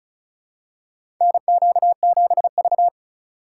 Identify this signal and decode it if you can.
Morse code, heard as a keyed tone, about 36 wpm, 700 Hz. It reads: NQ7V
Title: USA Call Signs - 4 Characters - 3x Character Spacing - 35wpm